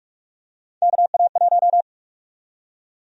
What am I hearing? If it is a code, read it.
Morse code, heard as a keyed tone, about 43 wpm, 700 Hz. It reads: KA1
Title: USA Call Sign Prefixes - 45wpm